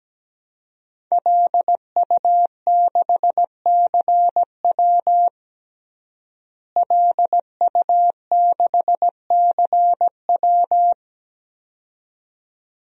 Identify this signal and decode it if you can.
Morse code, heard as a keyed tone, about 17 wpm, 700 Hz. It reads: LU6CW LU6CW
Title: DXCC-100 Call Signs - 5 Characters - Repeated 2x - 17wpm